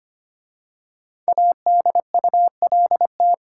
Morse code, heard as a keyed tone, about 25 wpm, 700 Hz. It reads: ADULT
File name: 5 Letter Words - First 3 Letters Given - 25wpm